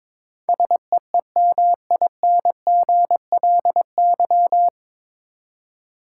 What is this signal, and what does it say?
Morse code, heard as a keyed tone, about 21 wpm, 700 Hz. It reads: SEEMINGLY
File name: 9 Letter Words - First 3 Letters Given - 22wpm